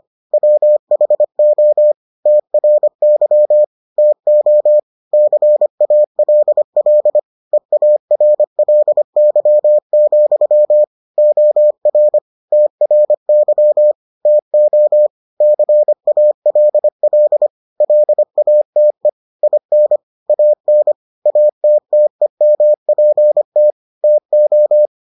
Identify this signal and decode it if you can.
Morse code, heard as a keyed tone, about 25 wpm, 600 Hz. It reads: WHO TRY TO CALL EARLY, OR TRY TO CALL LATE IN AN ATTEMPT TO